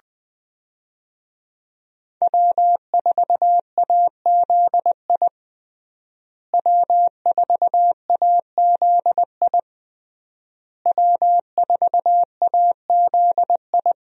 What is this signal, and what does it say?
Morse code, heard as a keyed tone, about 20 wpm, 700 Hz. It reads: W4AZI W4AZI W4AZI
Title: USA Call Signs - 5 Characters - Repeated 3x - 20wpm